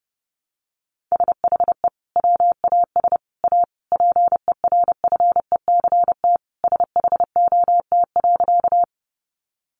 Morse code, heard as a keyed tone, about 30 wpm, 700 Hz. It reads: SHE WAS A PERFECT SHOT.
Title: Sentences from Top 1100 Words 30wpm